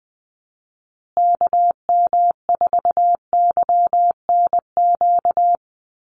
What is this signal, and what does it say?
Morse code, heard as a keyed tone, about 20 wpm, 700 Hz. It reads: KM4YNQ
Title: USA Call Signs Repeated 2x 20wpm